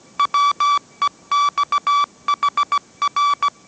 Morse code, heard as a keyed tone, about 17 wpm, 1150 Hz. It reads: WEXHR
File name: mrx / mrx080 18wpm